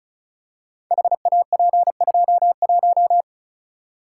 Morse code, heard as a keyed tone, about 36 wpm, 690 Hz. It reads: HAP21
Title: Code Groups - 5 Letter:Numbers - 35wpm